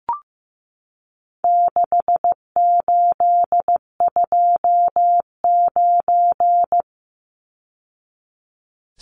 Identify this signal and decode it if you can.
Morse code, heard as a keyed tone, about 15 wpm, 700 Hz. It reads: 6829
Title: Numbers - 4-Digit 15wpm